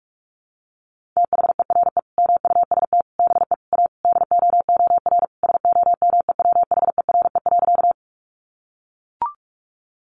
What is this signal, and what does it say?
Morse code, heard as a keyed tone, about 44 wpm, 700 Hz. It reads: THERE MUST BE A DOOR SOMEWHERE.
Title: Sentences from Top 1400 Words 45wpm